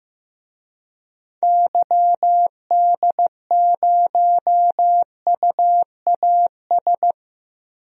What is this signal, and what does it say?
Morse code, heard as a keyed tone, about 16 wpm, 700 Hz. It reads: YD0UAS
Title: DXCC-100 Call Signs - 6 Characters - 15wpm